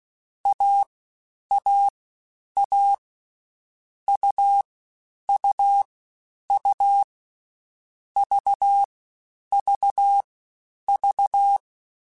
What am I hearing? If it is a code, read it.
Morse code, heard as a keyed tone, about 16 wpm, 800 Hz. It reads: A A A U U U V V V
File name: Difficult Combos / Difficult Combos-mp3s / AUV ShortRun @15